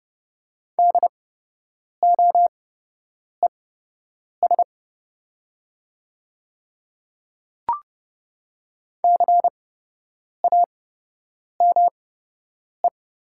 Morse code, heard as a keyed tone, about 30 wpm, spaced as for 8 wpm, 700 Hz. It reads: DOES CAME
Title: 8x Character Spacing - Top 500 Words - 30wpm